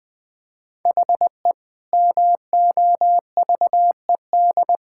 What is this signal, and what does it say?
Morse code, heard as a keyed tone, about 20 wpm, 700 Hz. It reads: HE MOVED
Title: Sentences with 3-Word Phrases - Spoken Beginning - 20wpm